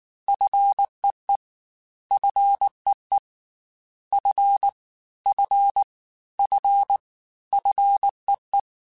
Morse code, heard as a keyed tone, about 19 wpm, 800 Hz. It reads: FEE FEE F F F FEE
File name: F and L / F and L mp3 / F and L Prefixes 3x2 Twice @18